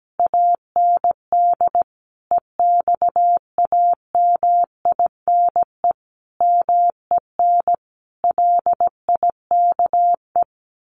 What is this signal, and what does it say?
Morse code, heard as a keyed tone, about 17 wpm, 700 Hz. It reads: AND EXAMINE MEN LIKE
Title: A Sound Of Thunder 17wpm